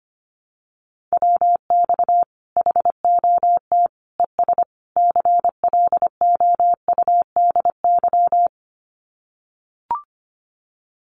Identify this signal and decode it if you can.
Morse code, heard as a keyed tone, about 25 wpm, 700 Hz. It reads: WX HOT ES CLOUDY
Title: Common QSO Phrases 25wpm